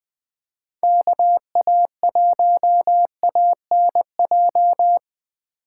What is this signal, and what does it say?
Morse code, heard as a keyed tone, about 21 wpm, 700 Hz. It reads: KA1ANJ
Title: Speed Racing - USA 2x3 Call Signs 17wpm